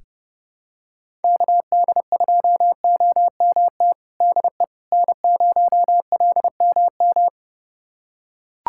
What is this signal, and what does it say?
Morse code, heard as a keyed tone, about 32 wpm, 700 Hz. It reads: KD2OMT DE N0LMM
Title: USA Callsign DE Callsign 30wpm